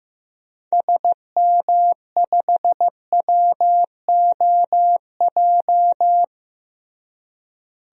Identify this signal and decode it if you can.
Morse code, heard as a keyed tone, about 15 wpm, 700 Hz. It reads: SM5WOJ
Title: DXCC-100 Call Signs - 6 Characters - Repeated 3x - 15wpm